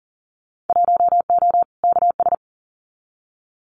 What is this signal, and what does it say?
Morse code, heard as a keyed tone, about 41 wpm, 700 Hz. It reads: 1O KS
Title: Winter Field Day Exchanges 40wpm